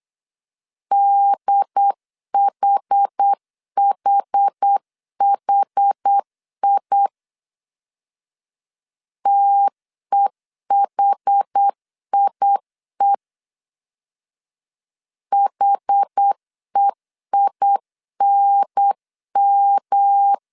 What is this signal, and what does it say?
Morse code, heard as a keyed tone, about 8 wpm, 800 Hz. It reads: DHHHI TEHIE HEINM
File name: CW / DET / 1 3 1500C S B 8wpm 4141